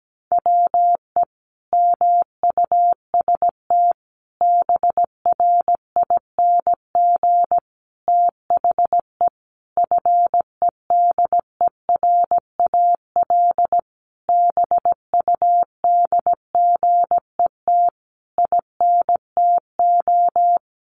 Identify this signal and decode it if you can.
Morse code, heard as a keyed tone, about 17 wpm, 700 Hz. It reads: WE MUST BRING THE FEDERAL BUDGET INTO
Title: Sentences from Top 1200 Words 17wpm